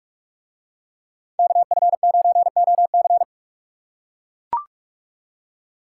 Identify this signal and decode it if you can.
Morse code, heard as a keyed tone, about 41 wpm, 700 Hz. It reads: KF9OC